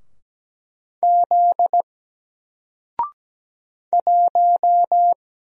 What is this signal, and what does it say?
Morse code, heard as a keyed tone, about 17 wpm, 700 Hz. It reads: Z 1
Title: Single Letter-Number Alt 17wpm